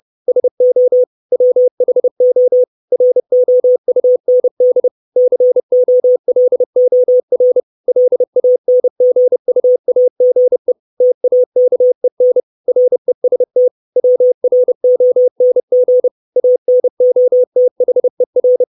Morse code, heard as a keyed tone, about 30 wpm, 500 Hz. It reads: SO WHO ROUND COLOR LANGUAGE TAKEN REST WRONG ANOTHER
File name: 632 Words 30WPM